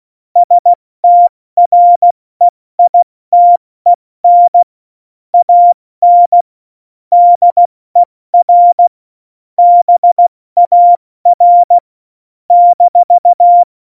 Morse code, heard as a keyed tone, about 16 wpm, 700 Hz. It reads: STREITEN AN DER BAR -
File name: Morsekurs / Witze / Witz 22. Oktober 2019, 70 F80